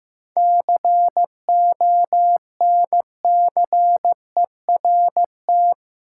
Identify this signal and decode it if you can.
Morse code, heard as a keyed tone, about 15 wpm, 700 Hz. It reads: CONCERT
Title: Seven Letter Words - 15wpm